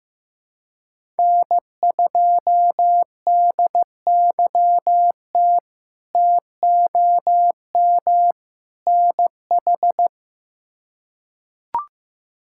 Morse code, heard as a keyed tone, about 15 wpm, 700 Hz. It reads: N2DYT TOM NH
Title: K1USN SST QSO Practice - 15wpm